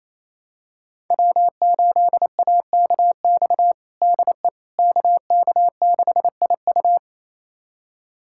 Morse code, heard as a keyed tone, about 28 wpm, 700 Hz. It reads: W8AKX DE KK6IU
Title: USA Callsign DE Callsign - 5 Characters - 28wpm